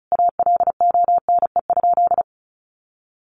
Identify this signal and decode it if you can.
Morse code, heard as a keyed tone, about 34 wpm, 700 Hz. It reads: ALONE?
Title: Sentences from Top 1000 Words 35wpm